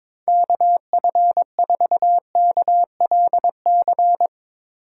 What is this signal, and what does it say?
Morse code, heard as a keyed tone, about 22 wpm, 700 Hz. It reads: KF4KLC
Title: USA Call Signs - 6 Characters - 8x Character Spacing - 22wpm